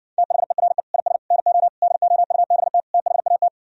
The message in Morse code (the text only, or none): THERE IS NO DOUBT THAT